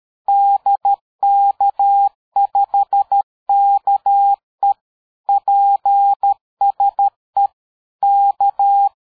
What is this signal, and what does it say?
Morse code, heard as a keyed tone, about 13 wpm, 800 Hz. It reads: DK5KE PSE K
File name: dk5ke / QSO 60BpM